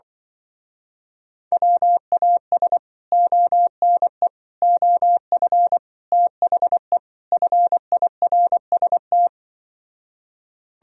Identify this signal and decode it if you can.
Morse code, heard as a keyed tone, about 24 wpm, 700 Hz. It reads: WAS ONE OF THE FIRST
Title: Speed-Racing - Sets of 5 Words 20wpm